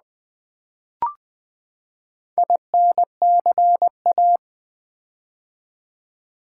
Morse code, heard as a keyed tone, about 20 wpm, 700 Hz. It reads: INCA
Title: Build A Word - 4 Letters At A Time - 11-Letter Words - 20wpm